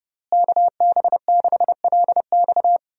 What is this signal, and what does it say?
Morse code, heard as a keyed tone, about 30 wpm, 700 Hz. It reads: KB6LX
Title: USA Call Signs - 5 Characters - Repeated 2x - 30wpm